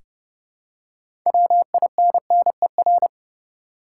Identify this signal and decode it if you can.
Morse code, heard as a keyed tone, about 30 wpm, 700 Hz. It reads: WINNER
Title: Six Letter Words - 30wpm